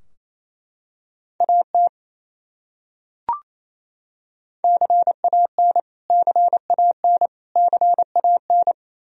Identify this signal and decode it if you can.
Morse code, heard as a keyed tone, about 28 wpm, 700 Hz. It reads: AT CAN CAN CAN
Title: Top 100 Words Repeated 28wpm